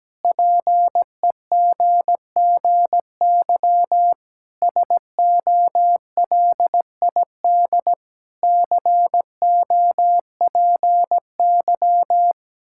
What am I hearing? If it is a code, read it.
Morse code, heard as a keyed tone, about 17 wpm, 700 Hz. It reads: PEGGY SOLID COPY
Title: Practice QSO - Exchange 3 - More Info - 17wpm